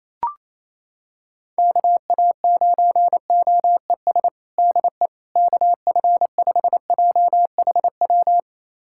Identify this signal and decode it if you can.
Morse code, heard as a keyed tone, about 28 wpm, 700 Hz. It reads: KA9OES DE KF5JHW
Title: USA Callsign DE Callsign 28wpm